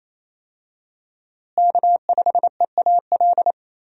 Morse code, heard as a keyed tone, about 27 wpm, 700 Hz. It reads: K5EAL
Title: USA Call Signs - 5 Characters - Repeated 3x - 28wpm